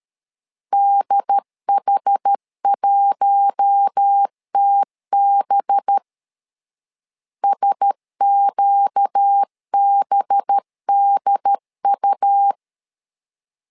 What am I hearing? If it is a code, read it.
Morse code, heard as a keyed tone, about 13 wpm, 800 Hz. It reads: DH1TB SQBDU
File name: LEZIONI / 5 LBVC / 5 LB 12wpm 1906